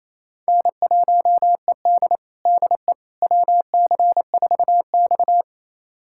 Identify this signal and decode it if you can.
Morse code, heard as a keyed tone, about 27 wpm, 700 Hz. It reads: N1ED DE WC4X
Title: USA Callsign DE Callsign - 4 Characters - 28wpm